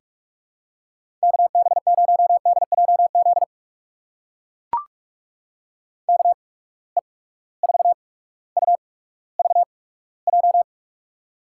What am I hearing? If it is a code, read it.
Morse code, heard as a keyed tone, about 44 wpm, spaced as for 19 wpm, 700 Hz. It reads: KB0DJZ KE4UVJ
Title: USA Call Signs - 6 Characters - 8x Character Spacing - 45wpm